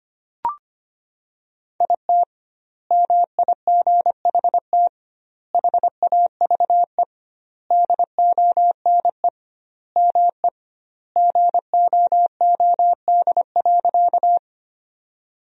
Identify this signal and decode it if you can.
Morse code, heard as a keyed tone, about 25 wpm, 700 Hz. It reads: IT MIGHT HAVE DONE ME GOOD.
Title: 2x Word-Spacing - Sentences from Top 500 Words - 25wpm